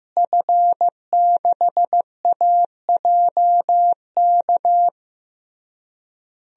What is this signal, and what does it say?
Morse code, heard as a keyed tone, about 15 wpm, 700 Hz. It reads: F6AJK
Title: DXCC-100 Call Signs - 5 Characters - Repeated 3x - 15wpm